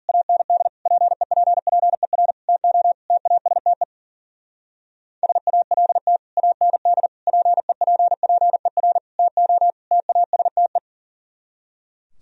Morse code, heard as a keyed tone, about 50 wpm, 700 Hz. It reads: AND PEPPER TO TASTE SALT AND PEPPER TO TASTE
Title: Speed-Racing - Sets of 5 Words 40wpm